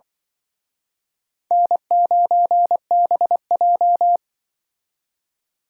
Morse code, heard as a keyed tone, about 24 wpm, 700 Hz. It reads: N9BJ
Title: Speed Racing - USA 1x2 Call Signs 20wpm